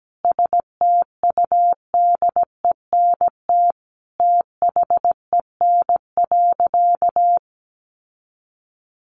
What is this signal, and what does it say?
Morse code, heard as a keyed tone, about 17 wpm, 700 Hz. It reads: STUDENT THEN.
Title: Sentences from Top 1000 Words 17wpm